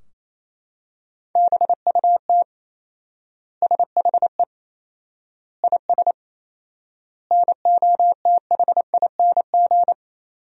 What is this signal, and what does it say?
Morse code, heard as a keyed tone, about 28 wpm, 700 Hz. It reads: BUT SHE IS NOTHING
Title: 4x Word-Spacing - Sentences from Top 500 Words - 28wpm